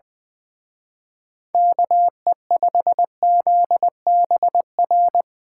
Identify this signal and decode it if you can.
Morse code, heard as a keyed tone, about 23 wpm, 700 Hz. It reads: KE5ZBR
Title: Speed Racing - USA 2x3 Call Signs 20wpm